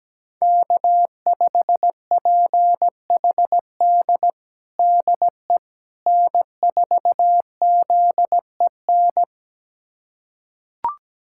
Morse code, heard as a keyed tone, about 17 wpm, 700 Hz. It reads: K5PHD DE N4ZEN
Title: USA Callsign DE Callsign - 5 Characters - 17wpm